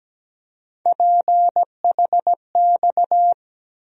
Morse code, heard as a keyed tone, about 17 wpm, 700 Hz. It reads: PHX
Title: USA Call Sign Postfixes - 3 Letters 17wpm